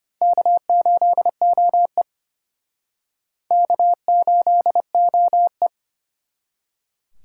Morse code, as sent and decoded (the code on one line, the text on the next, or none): -.- ---.. --- . / -.- ---.. --- .
K8OE K8OE